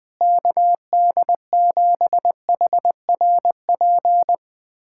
Morse code, heard as a keyed tone, about 20 wpm, 700 Hz. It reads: KD7HRP
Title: USA Callsign DE Callsign 20wpm